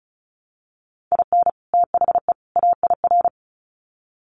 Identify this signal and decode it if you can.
Morse code, heard as a keyed tone, about 34 wpm, 700 Hz. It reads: IN THE AIR
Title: Sets of 3 Words 35wpm